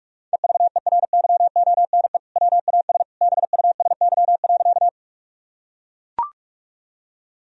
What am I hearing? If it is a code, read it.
Morse code, heard as a keyed tone, about 43 wpm, 700 Hz. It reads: EVERYONE WAS BUSY.
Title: Sentences from Top 1000 Words - Review Part 3 - 45wpm